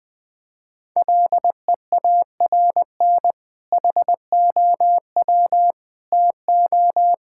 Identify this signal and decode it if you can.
Morse code, heard as a keyed tone, about 20 wpm, 700 Hz. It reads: LEARN HOW TO
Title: Sentences with 2-Word Phrases - Spoken Beginning - 20wpm